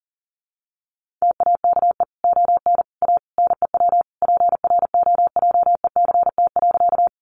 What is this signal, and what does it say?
Morse code, heard as a keyed tone, about 37 wpm, 700 Hz. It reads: TAKE ON A NEW PROJECT.
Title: Sentences from Top 1400 Words 40wpm